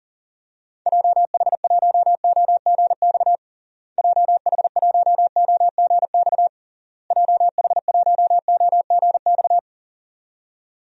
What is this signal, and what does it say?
Morse code, heard as a keyed tone, about 40 wpm, 700 Hz. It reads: JH1OGX JH1OGX JH1OGX
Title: DXCC-100 Call Signs - 6 Characters - Repeated 3x - 40wpm